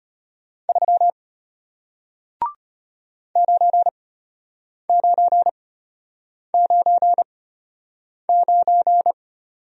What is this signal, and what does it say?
Morse code, heard as a keyed tone, about 33 wpm, spaced as for 8 wpm, 700 Hz. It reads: 3 9999